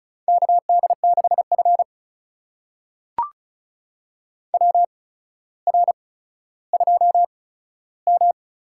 Morse code, heard as a keyed tone, about 36 wpm, spaced as for 15 wpm, 700 Hz. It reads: KD6F WR2M
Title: USA Call Signs - 4 Characters - 8x Character Spacing - 35wpm